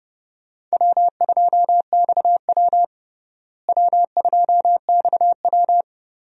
Morse code, heard as a keyed tone, about 30 wpm, 700 Hz. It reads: W2XW W2XW
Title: USA Call Signs - 4 Characters - Repeated 3x - 30wpm